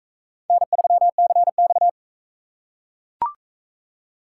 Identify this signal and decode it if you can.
Morse code, heard as a keyed tone, about 40 wpm, 700 Hz. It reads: N3CX